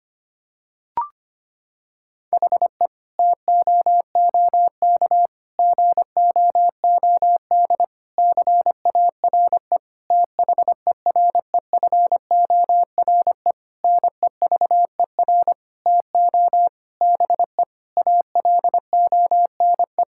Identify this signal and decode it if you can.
Morse code, heard as a keyed tone, about 25 wpm, 700 Hz. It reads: HE TOOK GOOD CARE THEREFORE NEVER TO BE ALONE
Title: Sentences from Top 1000 Words - Review Part 5 - 25wpm